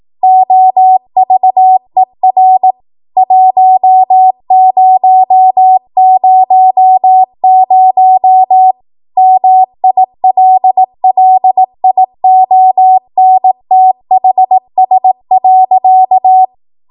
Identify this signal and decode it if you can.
Morse code, heard as a keyed tone, about 18 wpm, 750 Hz. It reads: OVER 1000 MILLIONTHS.
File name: Morse / 18WPM